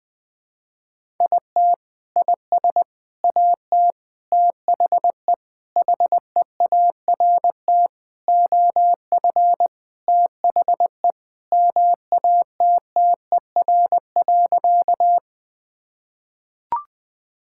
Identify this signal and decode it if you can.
Morse code, heard as a keyed tone, about 20 wpm, 700 Hz. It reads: IT IS AT THE HEART OF THE MATTER.
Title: Sentences with 4-Word Phrases - 20wpm